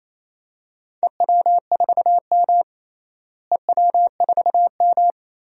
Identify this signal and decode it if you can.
Morse code, heard as a keyed tone, about 27 wpm, 700 Hz. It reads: EW4M EW4M
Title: DXCC-100 Call Signs - 4 Characters - Repeated 3x - 28wpm